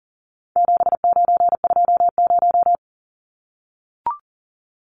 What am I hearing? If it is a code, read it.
Morse code, heard as a keyed tone, about 39 wpm, 700 Hz. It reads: 7920
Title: Numbers - 4-Digit 40wpm